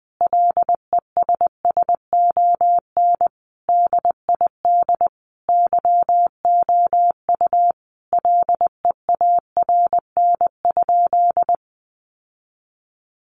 Morse code, heard as a keyed tone, about 20 wpm, 700 Hz. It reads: LESSON DID YOU LEARN?
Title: Sentences from Top 1200 Words 20wpm